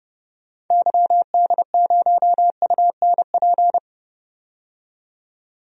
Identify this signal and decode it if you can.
Morse code, heard as a keyed tone, about 30 wpm, 700 Hz. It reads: YD0UNP
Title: DXCC-100 Call Signs - 6 Characters - 30wpm